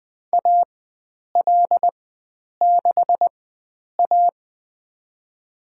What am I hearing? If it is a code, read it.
Morse code, heard as a keyed tone, about 20 wpm, spaced as for 9 wpm, 700 Hz. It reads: AL6A
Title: USA Call Signs - 4x Character Spacing - 20wpm